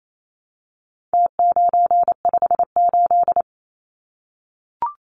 Morse code, heard as a keyed tone, about 28 wpm, 700 Hz. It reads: T958
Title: Numbers - Contest Style - 4-Digit 28wpm